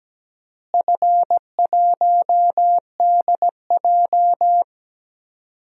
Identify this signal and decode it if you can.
Morse code, heard as a keyed tone, about 17 wpm, 700 Hz. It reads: F1DJ
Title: DXCC-100 Call Signs - 4 Characters - Repeated 3x - 17wpm